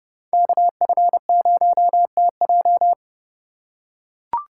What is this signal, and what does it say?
Morse code, heard as a keyed tone, about 30 wpm, 700 Hz. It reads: KF0TJ
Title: USA Call Signs - 5 Characters - 6x Character Spacing - 30wpm